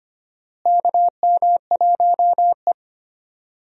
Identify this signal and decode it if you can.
Morse code, heard as a keyed tone, about 25 wpm, 700 Hz. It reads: KM1E
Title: USA Call Signs - 4 Characters - Repeated 2x - 25wpm